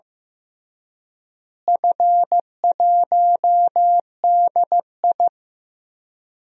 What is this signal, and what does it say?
Morse code, heard as a keyed tone, about 15 wpm, 700 Hz. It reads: F1DI
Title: DXCC-100 Call Signs - 4 Characters - Repeated 3x - 15wpm